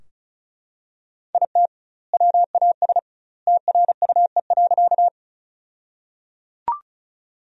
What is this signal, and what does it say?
Morse code, heard as a keyed tone, about 36 wpm, 700 Hz. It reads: IT WAS TRUE.
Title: 2x Word-Spacing - Sentences from Top 400 Words - 35wpm